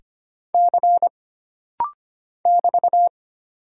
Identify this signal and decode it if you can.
Morse code, heard as a keyed tone, about 25 wpm, 700 Hz. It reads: C <BT>
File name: Single Character (Letter:Number:Punctuation:Prosign) - Rapid-Fire - 25wpm